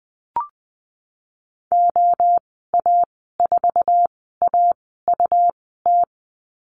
Morse code, heard as a keyed tone, about 20 wpm, spaced as for 15 wpm, 700 Hz. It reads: OA4AUT
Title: 140 Lesson 23g - Letter f - Call Signs 20wpm